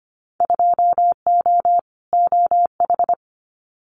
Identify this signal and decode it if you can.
Morse code, heard as a keyed tone, about 25 wpm, 700 Hz. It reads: 2O OH
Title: Winter Field Day Exchanges 25wpm